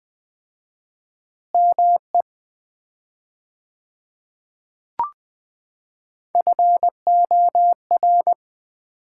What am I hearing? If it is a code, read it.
Morse code, heard as a keyed tone, about 20 wpm, 700 Hz. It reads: ME FOR
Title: 4x Word-Spacing - Sets of 2 Words - 20wpm